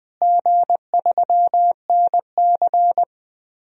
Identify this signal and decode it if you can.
Morse code, heard as a keyed tone, about 20 wpm, 700 Hz. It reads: G3NC
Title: DXCC-100 Call Signs - 4 Characters - Repeated 3x - 20wpm